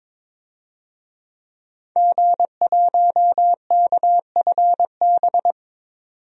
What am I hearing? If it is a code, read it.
Morse code, heard as a keyed tone, about 22 wpm, 700 Hz. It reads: G1KFB
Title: DXCC-100 Call Signs - 5 Characters - 22wpm